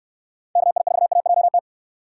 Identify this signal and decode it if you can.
Morse code, heard as a keyed tone, about 69 wpm, 700 Hz. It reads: KI4AJT